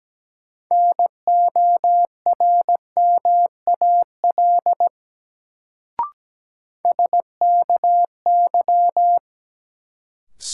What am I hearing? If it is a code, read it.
Morse code, heard as a keyed tone, about 17 wpm, 700 Hz. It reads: NORMAL SKY
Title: Sentences from Top 1200 Words - New Word Drill - 17wpm